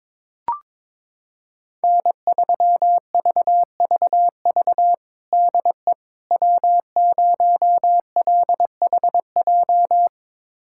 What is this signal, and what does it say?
Morse code, heard as a keyed tone, about 22 wpm, 700 Hz. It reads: N3VVV DE W0LHJ
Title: USA Callsign DE Callsign - 5 Characters - 22wpm